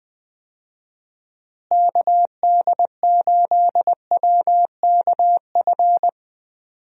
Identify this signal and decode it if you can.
Morse code, heard as a keyed tone, about 20 wpm, 700 Hz. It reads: KD8WKF
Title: USA Call Signs - 6 Characters - Repeated 3x - 20wpm